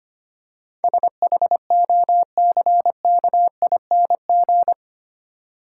Morse code, heard as a keyed tone, about 25 wpm, 700 Hz. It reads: SHOCKING
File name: Eight Letter Words - 25wpm